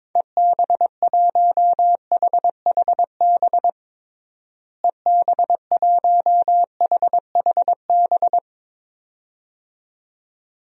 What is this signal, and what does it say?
Morse code, heard as a keyed tone, about 21 wpm, 700 Hz. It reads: EB1HHB EB1HHB
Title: DXCC-100 Call Signs - 6 Characters - Repeated 2x - 22wpm